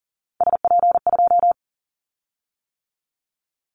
Morse code, heard as a keyed tone, about 40 wpm, 700 Hz. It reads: SP2
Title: Callsign Prefixes for Top 25 DXCC - 40wpm